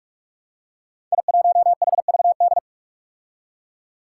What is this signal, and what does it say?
Morse code, heard as a keyed tone, about 44 wpm, 700 Hz. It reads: I1HVD
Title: Five Challenging Letter-Number Combinations 45wpm